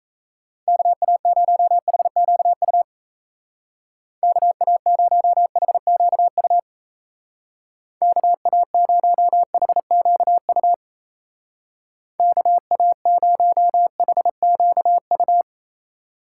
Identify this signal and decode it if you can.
Morse code, heard as a keyed tone, about 37 wpm, 700 Hz. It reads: KA0HQU KA0HQU KA0HQU KA0HQU